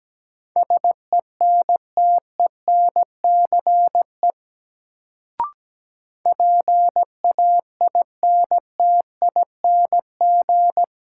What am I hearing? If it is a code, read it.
Morse code, heard as a keyed tone, about 17 wpm, 700 Hz. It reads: SENTENCE PAINTING
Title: Sentences from Top 1200 Words - New Word Drill Alt - 17wpm